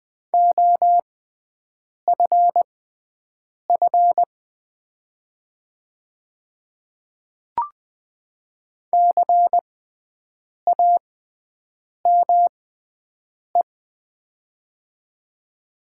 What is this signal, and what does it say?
Morse code, heard as a keyed tone, about 20 wpm, spaced as for 7 wpm, 690 Hz. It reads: OFF CAME
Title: 6x Character Spacing - Top 300 Words - 20wpm